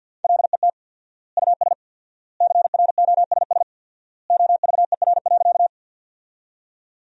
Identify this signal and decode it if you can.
Morse code, heard as a keyed tone, about 50 wpm, 700 Hz. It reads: LET US CROSS OVER.
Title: 4x Word-Spacing - Sentences from Top 300 Words - 50wpm